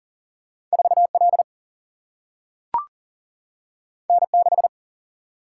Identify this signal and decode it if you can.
Morse code, heard as a keyed tone, about 40 wpm, 700 Hz. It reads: 4 L N 6
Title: Single-Letter Single-Number 40wpm